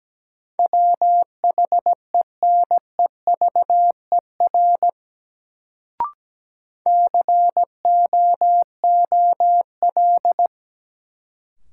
Morse code, heard as a keyed tone, about 17 wpm, 700 Hz. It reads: WHENEVER COOL
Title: Sentences from Top 1200 Words - New Word Drill Alt - 17wpm